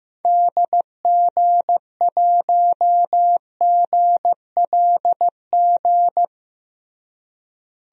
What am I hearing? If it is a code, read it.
Morse code, heard as a keyed tone, about 15 wpm, 700 Hz. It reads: DG1GLG
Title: DXCC-100 Call Signs - 6 Characters - Repeated 2x - 15wpm